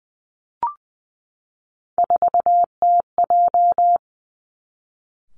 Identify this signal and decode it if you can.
Morse code, heard as a keyed tone, about 20 wpm, 700 Hz. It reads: 4TJ
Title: Three Challenging Letter-Number Combinations 20wpm